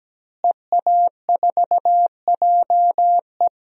Morse code, heard as a keyed tone, about 17 wpm, 700 Hz. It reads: EA4JE
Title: DXCC-100 Call Signs - 5 Characters - Repeated 2x - 17wpm